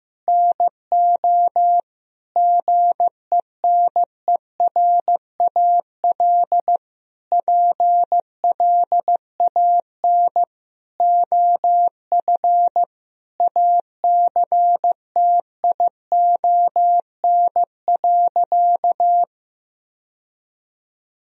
Morse code, heard as a keyed tone, about 15 wpm, 700 Hz. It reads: NO GENERAL PLAN OF ACTION.
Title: Sentences from Top 1000 Words - Review Part 4 - 15wpm